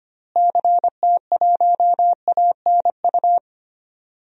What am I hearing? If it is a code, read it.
Morse code, heard as a keyed tone, about 25 wpm, 700 Hz. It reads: CT1ANU
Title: DXCC-100 Call Signs - 6 Characters - Repeated 3x - 25wpm